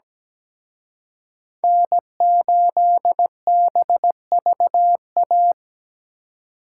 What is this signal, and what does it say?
Morse code, heard as a keyed tone, about 20 wpm, 700 Hz. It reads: N8BVA
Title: Speed Racing - USA 1x3 Call Signs 17wpm